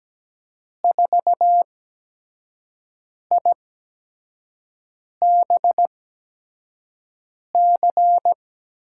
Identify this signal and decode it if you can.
Morse code, heard as a keyed tone, about 17 wpm, spaced as for 5 wpm, 700 Hz. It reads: 4IBC
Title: USA Call Signs - 6 Characters - 8x Character Spacing - 17wpm